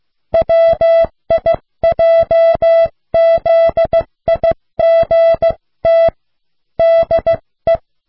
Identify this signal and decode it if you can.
Morse code, heard as a keyed tone, about 15 wpm, 660 Hz. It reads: WIJZIGT DE